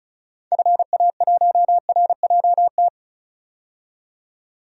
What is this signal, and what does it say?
Morse code, heard as a keyed tone, about 36 wpm, 700 Hz. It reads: FA1RJT
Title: DXCC-100 Call Signs - 6 Characters - 35wpm